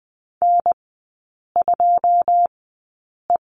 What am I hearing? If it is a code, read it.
Morse code, heard as a keyed tone, about 20 wpm, spaced as for 8 wpm, 700 Hz. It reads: N2E
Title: 127 Lesson 21e - Number 6 - Three Characters 20wpm